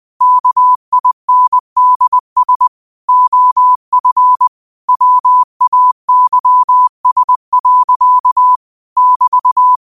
Morse code, heard as a keyed tone, about 20 wpm, 1000 Hz. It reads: KINDS OF WAYS. <BT>
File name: qotdfiles / 20wpm